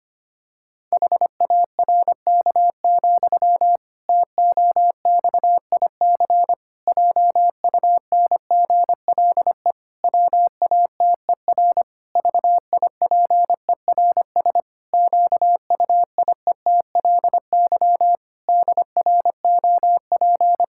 Morse code, heard as a keyed tone, about 25 wpm, 700 Hz. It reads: HARK, TOXIC JUNGLE WATER VIPERS QUIETLY DROP
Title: Pangram Challenge v2 - 25wpm